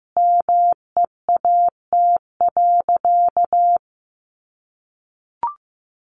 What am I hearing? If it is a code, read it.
Morse code, heard as a keyed tone, about 15 wpm, 700 Hz. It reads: MEAT.
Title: Sentences from Top 1500 Words 15wpm